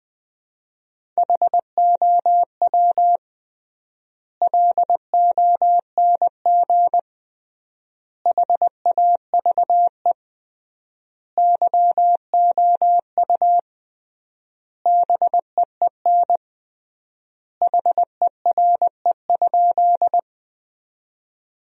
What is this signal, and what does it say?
Morse code, heard as a keyed tone, about 20 wpm, 700 Hz. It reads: HOW LONG HAVE YOU BEEN HERE?
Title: 3x Word-Spacing - Sentences from Top 200 Words - 20wpm